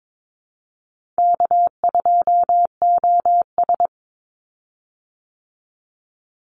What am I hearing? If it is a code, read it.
Morse code, heard as a keyed tone, about 22 wpm, 700 Hz. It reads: K2OS
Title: Call Signs Repeated 2x - 22wpm